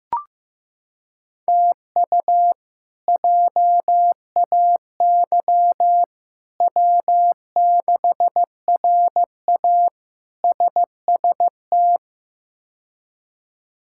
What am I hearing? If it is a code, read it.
Morse code, heard as a keyed tone, about 15 wpm, 700 Hz. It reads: TU JAY W6RA SST
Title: K1USN SST QSO Practice - 15wpm